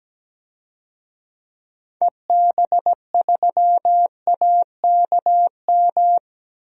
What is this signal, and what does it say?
Morse code, heard as a keyed tone, about 17 wpm, 700 Hz. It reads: EB3AKM
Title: DXCC-100 Call Signs - 6 Characters - Repeated 3x - 17wpm